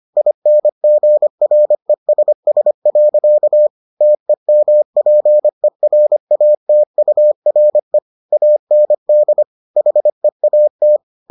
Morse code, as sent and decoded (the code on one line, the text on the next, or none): .. -. --. .-. ... ... .-.-.- / - . -- .--. .-. .- - ..- .-. / .- -. -.. / .... .- -
INGRESS. TEMPERATURE AND HEAT